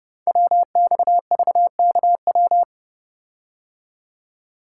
Morse code, heard as a keyed tone, about 28 wpm, 700 Hz. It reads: WXVKW
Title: Code Groups - 5 Letters - 30wpm